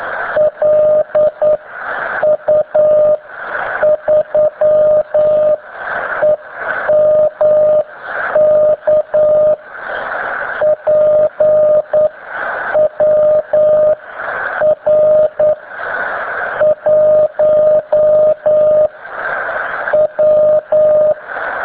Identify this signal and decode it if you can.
Morse code, heard as a keyed tone, about 9 wpm, spaced as for 7 wpm, 610 Hz, with radio noise. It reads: LU3EMK PWR 1 W